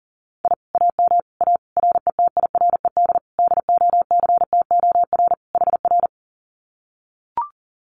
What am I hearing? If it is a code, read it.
Morse code, heard as a keyed tone, about 40 wpm, 700 Hz. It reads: I AM A RETIRED DOCTOR HR
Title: Common QSO Phrases Encore 40wpm